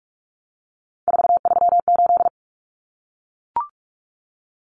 Morse code, heard as a keyed tone, about 44 wpm, 700 Hz. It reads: 438
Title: Numbers - 3-Digit 45wpm